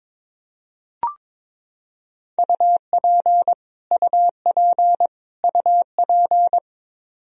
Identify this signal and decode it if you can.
Morse code, heard as a keyed tone, about 22 wpm, 700 Hz. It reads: UP UP UP
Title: Top 100 Words Repeated 22wpm